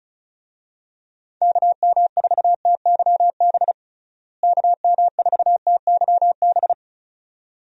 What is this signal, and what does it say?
Morse code, heard as a keyed tone, about 34 wpm, 700 Hz. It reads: KM4TYB KM4TYB
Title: USA Call Signs - 6 Characters - Repeated 2x - 35wpm